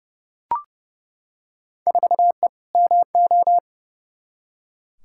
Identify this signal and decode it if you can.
Morse code, heard as a keyed tone, about 30 wpm, 700 Hz. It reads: 4E MO
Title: Field Day Exchanges Encore - 30wpm